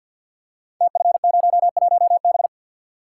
Morse code, heard as a keyed tone, about 52 wpm, 700 Hz. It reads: TF01B
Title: Code Groups - 5 Letter:Numbers - 50wpm